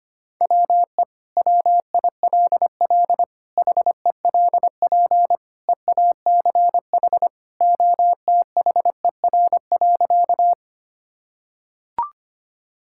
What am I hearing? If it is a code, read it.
Morse code, heard as a keyed tone, about 25 wpm, 700 Hz. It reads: WE WILL HELP EACH OTHER.
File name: practice / 25wpm / Sentences with 2-Word Phrases - 25wpm